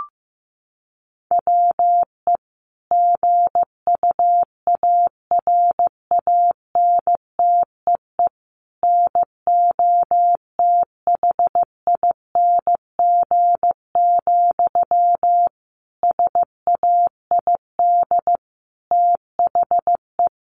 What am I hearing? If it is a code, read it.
Morse code, heard as a keyed tone, about 15 wpm, 700 Hz. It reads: WE GUARANTEE NOTHING, SAID THE
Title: A Sound Of Thunder 15wpm